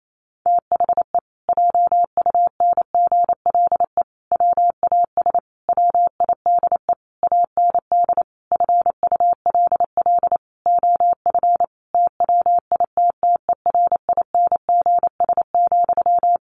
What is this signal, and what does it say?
Morse code, heard as a keyed tone, about 28 wpm, 700 Hz. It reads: THE JUNGLE WAS WIDE AND FULL OF TWITTERINGS,
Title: A Sound Of Thunder 28wpm